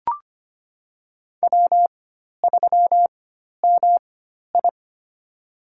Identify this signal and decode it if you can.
Morse code, heard as a keyed tone, about 25 wpm, spaced as for 12 wpm, 700 Hz. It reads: W3MI
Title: USA Call Signs - 4 Characters - 4x Character Spacing - 25wpm